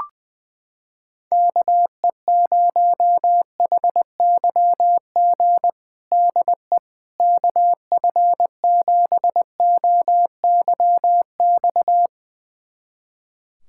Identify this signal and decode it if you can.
Morse code, heard as a keyed tone, about 20 wpm, 700 Hz. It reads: KE0HYG DE KF7OYX
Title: USA Callsign DE Callsign - 6 Characters - 20wpm